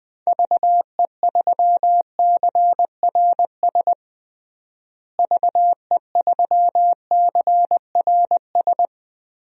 Morse code, heard as a keyed tone, about 20 wpm, 700 Hz. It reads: VE3CRS VE3CRS
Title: DXCC-100 Call Signs - 6 Characters - Repeated 2x - 20wpm